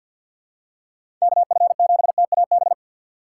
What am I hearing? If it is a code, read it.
Morse code, heard as a keyed tone, about 54 wpm, 700 Hz. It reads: KF7TAB
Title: USA Call Signs - 6 Characters - 6x Character Spacing - 50wpm